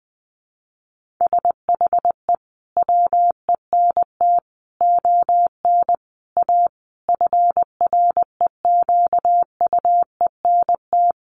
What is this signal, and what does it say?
Morse code, heard as a keyed tone, about 20 wpm, 700 Hz. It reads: SHE WENT ON A FREQUENT
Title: Sentences from Top 1400 Words 20wpm